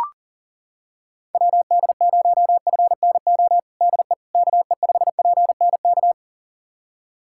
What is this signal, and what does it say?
Morse code, heard as a keyed tone, about 40 wpm, 700 Hz. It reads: WD0FNO DE KE5PNK
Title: USA Callsign DE Callsign - 6 Characters - 40wpm